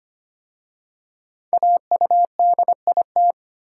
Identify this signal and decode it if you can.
Morse code, heard as a keyed tone, about 25 wpm, 700 Hz. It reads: AUDIT
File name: 5 Letter Words - First 3 Letters Given - 25wpm